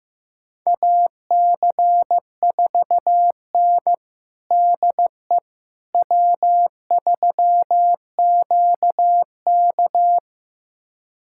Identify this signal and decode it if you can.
Morse code, heard as a keyed tone, about 15 wpm, 700 Hz. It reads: AC4N DE W3QK
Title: USA Callsign DE Callsign - 4 Characters - 15wpm